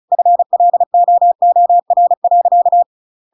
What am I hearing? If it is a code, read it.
Morse code, heard as a keyed tone, about 34 wpm, 700 Hz. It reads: FLOOR.
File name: Pinocchio / Pinocchio ch7 35 35 ik1hge